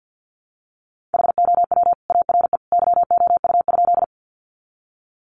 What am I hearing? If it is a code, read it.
Morse code, heard as a keyed tone, about 50 wpm, 700 Hz. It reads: HOW ARE YOU?
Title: Sentences from Top 100 Words 50wpm